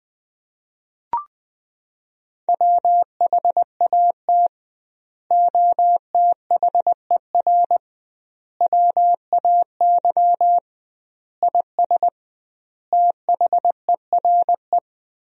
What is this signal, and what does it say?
Morse code, heard as a keyed tone, about 20 wpm, 700 Hz. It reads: WHAT OTHER WAY IS THERE
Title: 2x Word-Spacing - Sentences from Top 100 Words - 20wpm